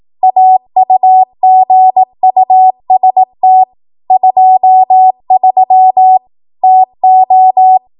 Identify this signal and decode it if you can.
Morse code, heard as a keyed tone, about 18 wpm, 750 Hz. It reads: AUGUST 23 TO
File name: Archive / 18 WPM / 18WPM